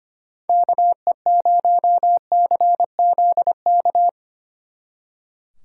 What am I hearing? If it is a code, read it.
Morse code, heard as a keyed tone, about 32 wpm, 700 Hz. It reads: KE0CZK